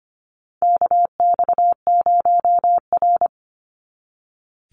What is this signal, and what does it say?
Morse code, heard as a keyed tone, about 25 wpm, 690 Hz. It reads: KX0R
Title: NA SOTA Activator Callsigns 25wpm